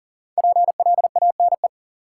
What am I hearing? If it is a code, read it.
Morse code, heard as a keyed tone, about 40 wpm, 700 Hz. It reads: PLANE
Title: Sentences from Top 1000 Words - Review Part 3 - New Word Drill - 40wpm